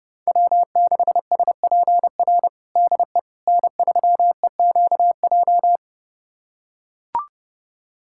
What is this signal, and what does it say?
Morse code, heard as a keyed tone, about 32 wpm, 700 Hz. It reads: W6SPR DE N3EQJ
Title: USA Callsign DE Callsign - 5 Characters - 30wpm